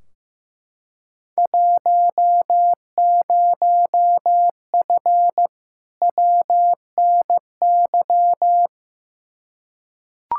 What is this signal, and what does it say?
Morse code, heard as a keyed tone, about 15 wpm, 700 Hz. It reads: 10F WNY
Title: Field Day Exchanges Encore - 15wpm